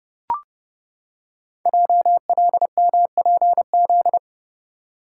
Morse code, heard as a keyed tone, about 30 wpm, 700 Hz. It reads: JLMPZ
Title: Code Groups - 5 Letters - 30wpm